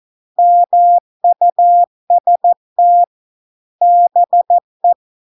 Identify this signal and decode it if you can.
Morse code, heard as a keyed tone, about 14 wpm, 700 Hz. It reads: MUST BE